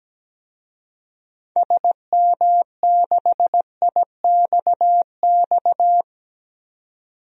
Morse code, heard as a keyed tone, about 17 wpm, 700 Hz. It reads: SM6IXX
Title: DXCC-100 Call Signs - 6 Characters - 17wpm